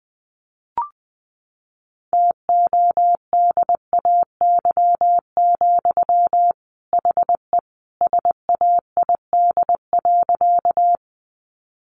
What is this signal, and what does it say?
Morse code, heard as a keyed tone, about 19 wpm, 700 Hz. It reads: TODAY, HE SAID.
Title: Sentences from Top 600 Words 20wpm